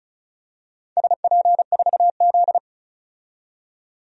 Morse code, heard as a keyed tone, about 36 wpm, 700 Hz. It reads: SP4Z
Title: DXCC-100 Call Signs - 4 Characters - 35wpm